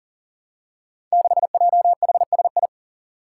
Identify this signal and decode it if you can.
Morse code, heard as a keyed tone, about 39 wpm, 700 Hz. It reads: 6JHSI
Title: Five Challenging Letter-Number Combinations 40wpm